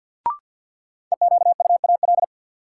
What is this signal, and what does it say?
Morse code, heard as a keyed tone, about 45 wpm, 700 Hz. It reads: EQUAL
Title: Sentences from Top 1400 Words - New Word Drill - 50wpm